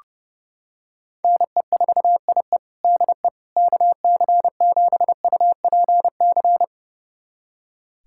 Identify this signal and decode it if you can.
Morse code, heard as a keyed tone, about 29 wpm, 700 Hz. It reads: NE4IE DE KC7UPC
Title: USA Callsign DE Callsign 30wpm